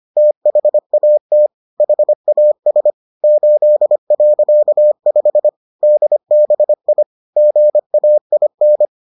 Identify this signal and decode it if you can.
Morse code, heard as a keyed tone, about 25 wpm, 600 Hz. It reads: THAT HAS 8.5 DBI GAIN